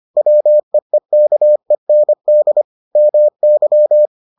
Morse code, heard as a keyed tone, about 25 wpm, 600 Hz. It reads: WEEKEND MY